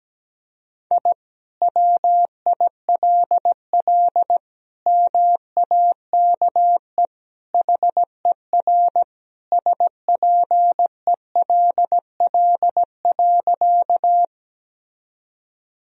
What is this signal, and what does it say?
Morse code, heard as a keyed tone, about 17 wpm, 700 Hz. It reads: I WILL MAKE HER SPELL.
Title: Sentences from Top 1000 Words - Review Part 1 - 17wpm